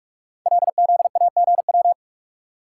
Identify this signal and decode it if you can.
Morse code, heard as a keyed tone, about 44 wpm, 700 Hz. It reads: LZAGW
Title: Code Groups - 5 Letter:Numbers - 45wpm